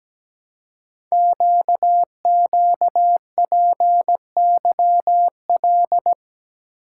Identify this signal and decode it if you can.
Morse code, heard as a keyed tone, about 17 wpm, 700 Hz. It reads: QQPYL
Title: Code Groups - 5 Letters - 17wpm